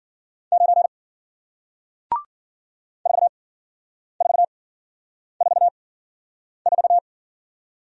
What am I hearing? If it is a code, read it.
Morse code, heard as a keyed tone, about 55 wpm, spaced as for 9 wpm, 700 Hz. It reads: 9 4444